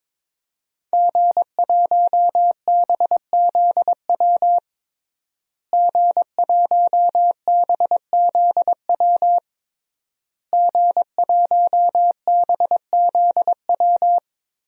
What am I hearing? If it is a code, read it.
Morse code, heard as a keyed tone, about 22 wpm, 700 Hz. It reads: G1BZW G1BZW G1BZW
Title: DXCC-100 Call Signs - 5 Characters - Repeated 3x - 22wpm